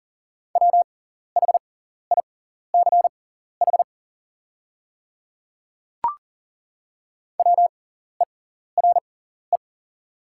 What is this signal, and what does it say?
Morse code, heard as a keyed tone, about 40 wpm, spaced as for 14 wpm, 700 Hz. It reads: WHICH WERE
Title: 6x Character Spacing - Top 200 Words - 40wpm